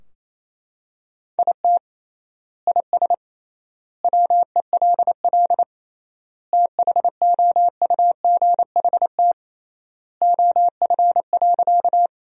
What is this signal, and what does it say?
Morse code, heard as a keyed tone, about 27 wpm, 700 Hz. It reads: IT IS WELL THOUGHT OF.
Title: 3x Word-Spacing - Sentences from Top 300 Words - 28wpm